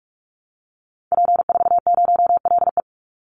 Morse code, heard as a keyed tone, about 44 wpm, 700 Hz. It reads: P40LE
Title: Call Signs Repeated 3x - 45wpm